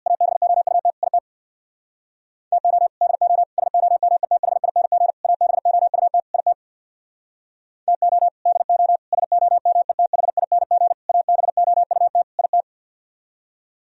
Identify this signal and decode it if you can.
Morse code, heard as a keyed tone, about 64 wpm, 700 Hz. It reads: ABOUT IT TO DO SOMETHING ABOUT IT TO DO SOMETHING ABOUT IT